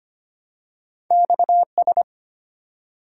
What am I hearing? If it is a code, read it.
Morse code, heard as a keyed tone, about 25 wpm, 700 Hz. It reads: XS
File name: Two Letter-Number Combinations - 25wpm